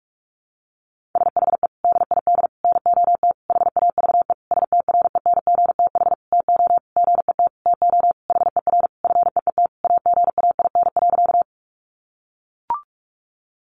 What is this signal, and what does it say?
Morse code, heard as a keyed tone, about 44 wpm, 700 Hz. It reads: SHE DID NOT HAVE STRENGTH TO GET TO HER FEET AGAIN.
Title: Sentences from Top 1300 Words 45wpm